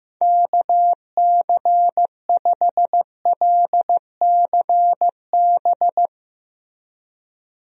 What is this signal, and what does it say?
Morse code, heard as a keyed tone, about 16 wpm, 700 Hz. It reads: KC5LCB
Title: USA Call Signs - 6 Characters - Repeated 2x - 15wpm